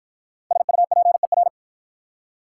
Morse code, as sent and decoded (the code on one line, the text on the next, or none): ... ..- .--. .-.
SUPER